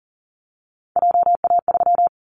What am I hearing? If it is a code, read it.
Morse code, heard as a keyed tone, about 40 wpm, 700 Hz. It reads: JA3
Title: Callsign Prefixes for Top 25 DXCC - Repeated 3x - 40wpm